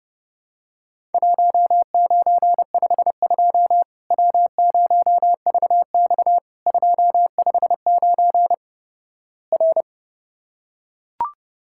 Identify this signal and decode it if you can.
Morse code, heard as a keyed tone, about 30 wpm, 700 Hz. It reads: 1952 W0VX 259 R
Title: Worked All Europe DX Contest - QTC Exchange - 30wpm